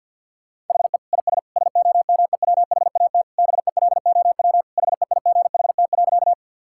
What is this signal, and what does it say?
Morse code, heard as a keyed tone, about 52 wpm, 700 Hz. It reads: HE IS SOMEWHAT BELOW HEIGHT.
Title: Sentences from Top 1600 Words 50wpm